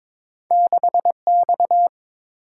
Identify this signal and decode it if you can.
Morse code, heard as a keyed tone, about 30 wpm, 700 Hz. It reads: 6 X